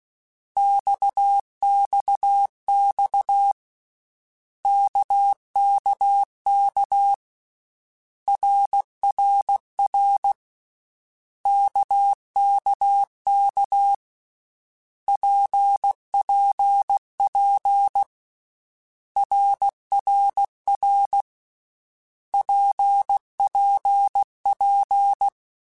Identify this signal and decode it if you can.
Morse code, heard as a keyed tone, about 16 wpm, 800 Hz. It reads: XXX KKK RRR KKK PPP RRR PPP
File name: Basic Letter Groups By7Groups / RKPX / RKPX@13